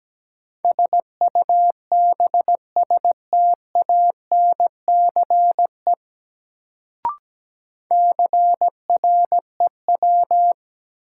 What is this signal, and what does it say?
Morse code, heard as a keyed tone, about 17 wpm, 700 Hz. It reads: SUBSTANCE CREW
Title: Sentences from Top 1500 Words - New Word Drill Alt - 17wpm